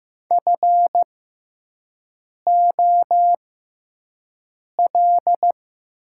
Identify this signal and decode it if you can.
Morse code, heard as a keyed tone, about 15 wpm, spaced as for 5 wpm, 700 Hz. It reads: FOL
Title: 6x Character Spacing - Top 500 Words - 15wpm